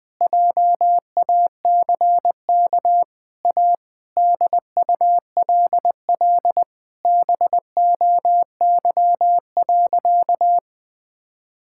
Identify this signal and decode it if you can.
Morse code, heard as a keyed tone, about 20 wpm, 700 Hz. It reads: JACK A DULL BOY.
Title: Sentences with 8-Word Phrases - 20wpm